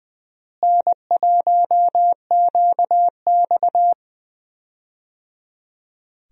Morse code, heard as a keyed tone, about 20 wpm, 700 Hz. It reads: N1QX
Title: USA Call Signs - 4 Characters - Repeated 3x - 20wpm